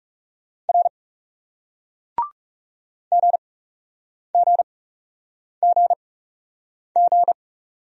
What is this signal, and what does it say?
Morse code, heard as a keyed tone, about 40 wpm, spaced as for 8 wpm, 700 Hz. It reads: R GGGG